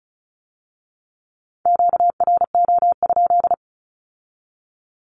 Morse code, heard as a keyed tone, about 36 wpm, 700 Hz. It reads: QRO?
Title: Q-Codes with Spoken Meaning - 35wpm